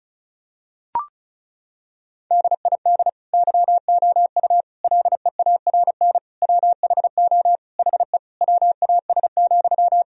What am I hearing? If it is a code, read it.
Morse code, heard as a keyed tone, about 34 wpm, 700 Hz. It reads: DID YOU LEARN WHO HE WAS,
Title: Sentences from Top 1000 Words - Review Part 1 - 35wpm